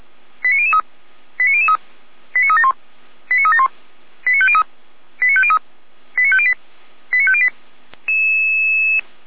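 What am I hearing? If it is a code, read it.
Five-tone selective call (ZVEI), heard as quick runs of five tones, tones of about 70 ms. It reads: 89003 89003 89371 89371 89503 89503 89508 89508
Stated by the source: ZVEI received on SDR, from the audio server of the Feuerwehr Orlamünde (fire brigade).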